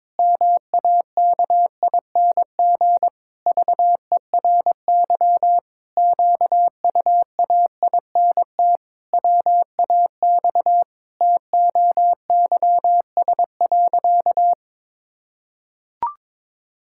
Morse code, heard as a keyed tone, about 22 wpm, 700 Hz. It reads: MAKING VERY QUAINT WAX TOYS.
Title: Pangram Challenge v2 - 22wpm